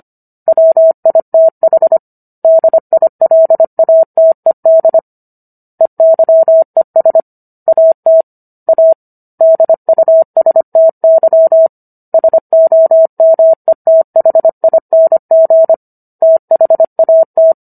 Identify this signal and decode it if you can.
Morse code, heard as a keyed tone, about 25 wpm, 650 Hz. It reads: WITH DILATED EYES AT A DUSTY SOMETHING THAT